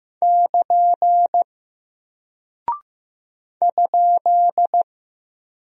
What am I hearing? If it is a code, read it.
Morse code, heard as a keyed tone, about 15 wpm, 700 Hz. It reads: <KN> ?
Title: Single Character (Letter:Number:Punctuation:Prosign) Alt 15wpm